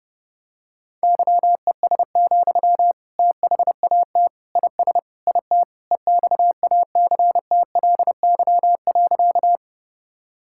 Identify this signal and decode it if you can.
Morse code, heard as a keyed tone, about 32 wpm, 700 Hz. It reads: YES, THAT IS IT EXACTLY.
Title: Sentences from Top 1000 Words - Review Part 6 - 30wpm